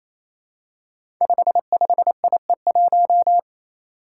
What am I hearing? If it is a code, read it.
Morse code, heard as a keyed tone, about 27 wpm, 700 Hz. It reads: 55IE1
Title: Five Challenging Letter-Number Combinations 28wpm